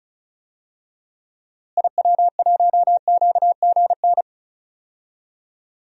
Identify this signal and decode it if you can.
Morse code, heard as a keyed tone, about 34 wpm, 700 Hz. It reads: IW1QGN
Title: DXCC-100 Call Signs - 6 Characters - Repeated 2x - 35wpm